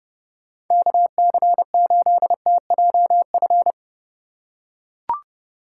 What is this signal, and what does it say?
Morse code, heard as a keyed tone, about 29 wpm, 700 Hz. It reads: KC8TJF
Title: USA Call Signs - 6 Characters - 3x Character Spacing - 30wpm